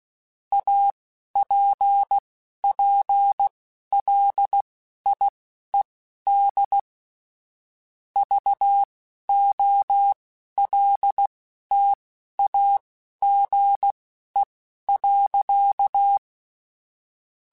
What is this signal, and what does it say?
Morse code, heard as a keyed tone, about 16 wpm, spaced as for 12 wpm, 800 Hz. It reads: APPLIED VOLTAGE.
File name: Technical Subjects / Technical Subjects-mp3s / XL @10